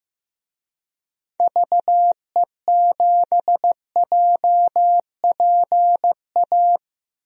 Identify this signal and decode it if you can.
Morse code, heard as a keyed tone, about 16 wpm, 700 Hz. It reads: VE7JPA
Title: DXCC-100 Call Signs - 6 Characters - Repeated 2x - 15wpm